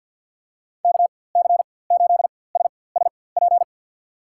Morse code, heard as a keyed tone, about 47 wpm, spaced as for 23 wpm, 700 Hz. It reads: KC8SSP
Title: USA Call Signs - 6 Characters - 4x Character Spacing - 50wpm